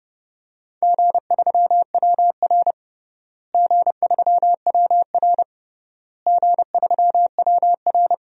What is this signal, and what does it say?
Morse code, heard as a keyed tone, about 32 wpm, 700 Hz. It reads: G3WR G3WR G3WR
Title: DXCC-100 Call Signs - 4 Characters - Repeated 3x - 30wpm